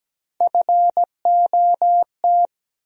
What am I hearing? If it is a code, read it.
Morse code, heard as a keyed tone, about 17 wpm, 700 Hz. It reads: FOT
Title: USA Call Sign Postfixes - 3 Letters 17wpm